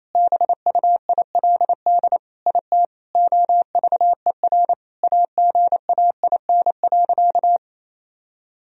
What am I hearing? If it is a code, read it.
Morse code, heard as a keyed tone, about 28 wpm, 700 Hz. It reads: BUILD IT OVER AGAIN.
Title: Sentences from Top 1000 Words - Review Part 1 - 28wpm